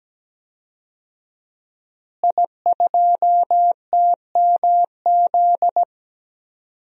Morse code, heard as a keyed tone, about 17 wpm, 700 Hz. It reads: I2TMZ
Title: DXCC-100 Call Signs - 5 Characters - Repeated 3x - 17wpm